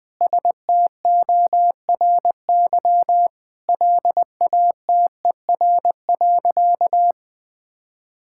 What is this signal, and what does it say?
Morse code, heard as a keyed tone, about 20 wpm, 700 Hz. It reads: STORY LATER.
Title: Sentences with 4-Word Phrases - Spoken Beginning - 20wpm